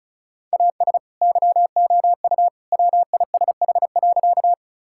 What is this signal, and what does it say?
Morse code, heard as a keyed tone, about 36 wpm, 700 Hz. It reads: AS YOU WISH.
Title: Sentences from Top 1600 Words 35wpm